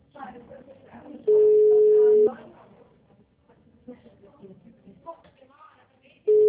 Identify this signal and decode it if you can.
Ringback tone (Europe, 425 Hz), with speech.